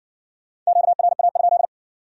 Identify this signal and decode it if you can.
Morse code, heard as a keyed tone, about 53 wpm, 700 Hz. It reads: QRA?